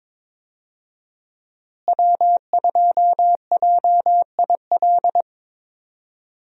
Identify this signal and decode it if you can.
Morse code, heard as a keyed tone, about 22 wpm, 700 Hz. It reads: W2JIL
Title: USA Call Signs - 5 Characters - Repeated 3x - 22wpm